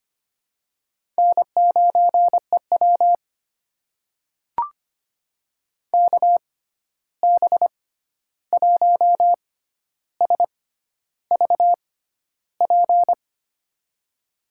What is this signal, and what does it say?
Morse code, heard as a keyed tone, about 25 wpm, spaced as for 11 wpm, 700 Hz. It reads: N9EW KB1SVP
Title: USA Call Signs - 6x Character Spacing - 25wpm